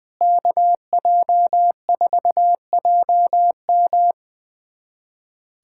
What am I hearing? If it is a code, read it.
Morse code, heard as a keyed tone, about 20 wpm, 700 Hz. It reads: KJ4JM
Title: USA Callsign DE Callsign - 5 Characters - 20wpm